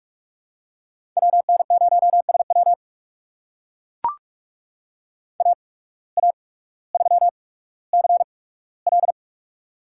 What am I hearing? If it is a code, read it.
Morse code, heard as a keyed tone, about 44 wpm, spaced as for 19 wpm, 700 Hz. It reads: WN0SW AA3CL
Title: USA Call Signs - 5 Characters - 8x Character Spacing - 45wpm